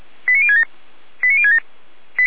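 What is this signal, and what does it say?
Five-tone selective call (ZVEI), heard as quick runs of five tones, tones of about 70 ms. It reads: 89067 89067
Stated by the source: ZVEI five-tone call received by SDR, archived by Feuerwehr Orlamünde.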